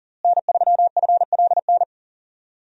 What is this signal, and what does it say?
Morse code, heard as a keyed tone, about 37 wpm, 700 Hz. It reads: N3FLN
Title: USA Call Signs - 5 Characters - Repeated 2x - 40wpm